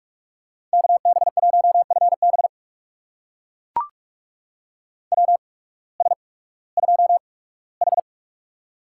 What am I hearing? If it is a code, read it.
Morse code, heard as a keyed tone, about 44 wpm, 690 Hz. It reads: K B 1 F B W S 2 H
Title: USA Call Signs - 8x Character Spacing - 45wpm